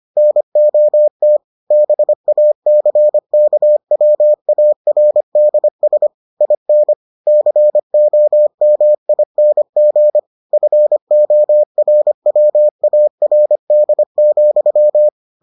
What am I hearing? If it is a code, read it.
Morse code, heard as a keyed tone, about 25 wpm, 600 Hz. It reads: NOT BACKWARDS IN COMING FORWARD,